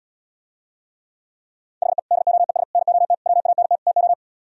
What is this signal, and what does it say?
Morse code, heard as a keyed tone, about 77 wpm, 700 Hz. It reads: HE DOES NOT WANT TO